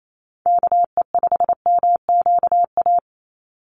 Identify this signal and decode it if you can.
Morse code, heard as a keyed tone, about 28 wpm, 700 Hz. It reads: KE5MQA
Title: USA Call Signs 28wpm